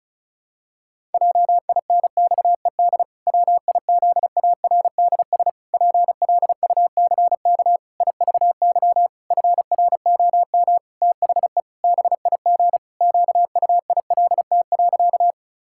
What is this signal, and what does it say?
Morse code, heard as a keyed tone, about 36 wpm, 700 Hz. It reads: JINXED WIZARDS PLUCK IVY FROM THE BIG QUILT.
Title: Pangram Challenge v2 - 35wpm